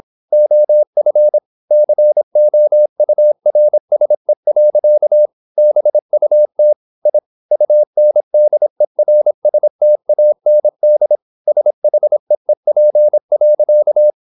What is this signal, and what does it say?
Morse code, heard as a keyed tone, about 26 wpm, 600 Hz. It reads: OF COURSE. BUT I UNDERSTAND SHEEP.
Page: Never 26wpm | CW med Gnister